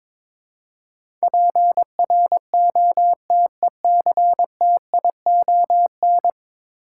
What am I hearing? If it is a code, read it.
Morse code, heard as a keyed tone, about 22 wpm, 700 Hz. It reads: PROTECTION
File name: practice / 22wpm / 10 Letter Words - First 3 Letters Given - 22wpm